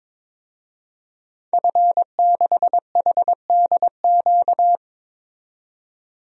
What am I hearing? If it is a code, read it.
Morse code, heard as a keyed tone, about 21 wpm, 700 Hz. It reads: F6HDQ
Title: DXCC-100 Call Signs - 5 Characters - 22wpm